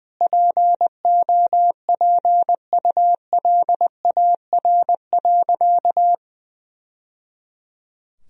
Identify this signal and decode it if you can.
Morse code, heard as a keyed tone, about 20 wpm, 700 Hz. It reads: POPULAR.
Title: Sentences from Top 1000 Words - Review Part 4 - 20wpm